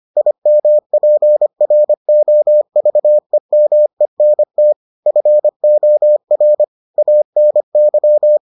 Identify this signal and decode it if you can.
Morse code, heard as a keyed tone, about 25 wpm, 600 Hz. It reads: IMPROVEMENT FOR ANY